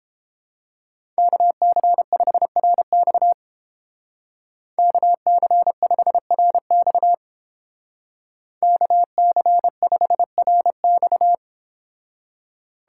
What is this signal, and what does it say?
Morse code, heard as a keyed tone, about 29 wpm, 700 Hz. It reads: KC5RX KC5RX KC5RX